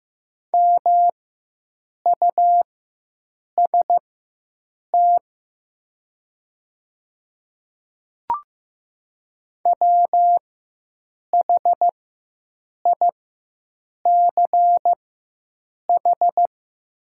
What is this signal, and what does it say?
Morse code, heard as a keyed tone, about 15 wpm, spaced as for 7 wpm, 700 Hz. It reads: MUST WHICH
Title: 4x Character Spacing - Top 200 Words - 15wpm